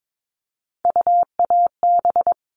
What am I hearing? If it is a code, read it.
Morse code, heard as a keyed tone, about 22 wpm, 700 Hz. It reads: UAB
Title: All Cut Numbers - 3-Digit 22wpm